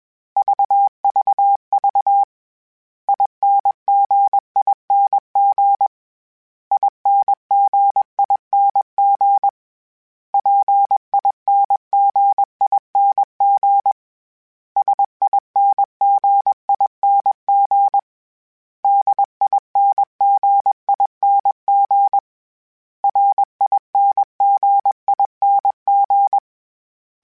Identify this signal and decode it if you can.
Morse code, heard as a keyed tone, about 21 wpm, 800 Hz. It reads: VVV INGING INGING PINGING SINGING DINGING RINGING